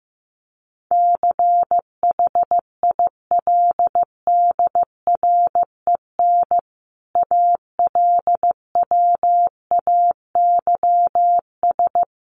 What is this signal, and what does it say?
Morse code, heard as a keyed tone, about 14 wpm, 700 Hz. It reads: CHILDREN ALWAYS
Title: Sentences from Top 1300 Words 15wpm